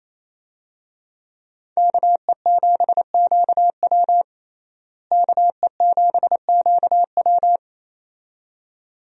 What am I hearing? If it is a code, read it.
Morse code, heard as a keyed tone, about 28 wpm, 700 Hz. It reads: KE7QW KE7QW
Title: USA Call Signs - 5 Characters - Repeated 2x - 28wpm